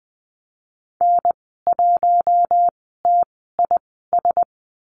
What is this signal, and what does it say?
Morse code, heard as a keyed tone, about 20 wpm, spaced as for 14 wpm, 700 Hz. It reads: N1TIS
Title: 048 Lesson 9g - Number 4 - Call Signs 20wpm